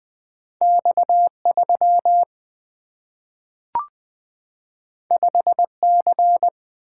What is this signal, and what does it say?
Morse code, heard as a keyed tone, about 20 wpm, 700 Hz. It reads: X 3 5 C
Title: Single-Letter Single-Number 20wpm